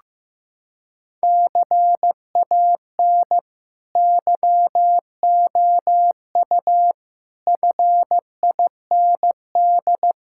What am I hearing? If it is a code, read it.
Morse code, heard as a keyed tone, about 16 wpm, 700 Hz. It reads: CAN YOU FIND
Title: Sentences from Top 1600 Words 15wpm